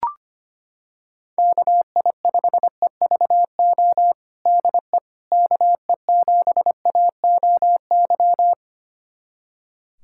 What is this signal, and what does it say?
Morse code, heard as a keyed tone, about 25 wpm, 700 Hz. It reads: KI5EVO DE KE7AOY
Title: USA Callsign DE Callsign 25wpm